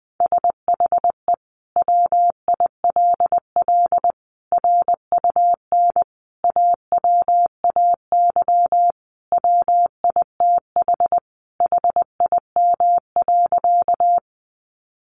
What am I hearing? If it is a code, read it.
Morse code, heard as a keyed tone, about 20 wpm, 700 Hz. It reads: SHE WILL RUN AWAY WITH HIM.
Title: Sentences from Top 600 Words 20wpm